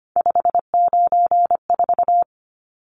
Morse code, heard as a keyed tone, about 25 wpm, 700 Hz. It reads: 594
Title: Numbers - 3-Digit 25wpm